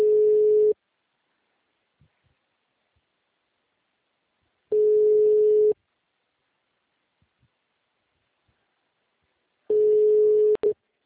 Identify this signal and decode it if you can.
Ringback tone (Europe, 425 Hz).
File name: out- -2016- - - .91